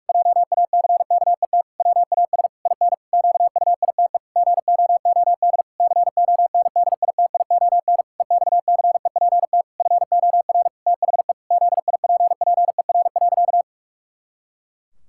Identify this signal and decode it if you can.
Morse code, heard as a keyed tone, about 46 wpm, 700 Hz. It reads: JACKET WAS IN QUITE GOOD CONDITION EXCEPT FOR THE ZIPPER.
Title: Pangram Challenge v2 - 45wpm